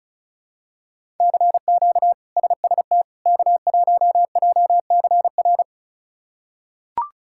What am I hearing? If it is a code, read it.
Morse code, heard as a keyed tone, about 36 wpm, 700 Hz. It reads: CQ SST K1JCR
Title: K1USN SST QSO Practice - 35wpm